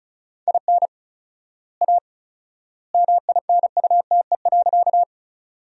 Morse code, heard as a keyed tone, about 34 wpm, 700 Hz. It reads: IN A MINUTE.
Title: 4x Word-Spacing - Sentences from Top 500 Words - 35wpm